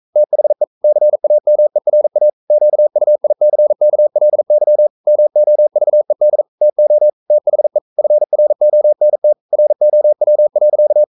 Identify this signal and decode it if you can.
Morse code, heard as a keyed tone, about 42 wpm, 600 Hz. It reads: THE CAMERA QUICKLY MOVED TO THE FRONT ROW.
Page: Romanen Never af Ken Follet på engelsk.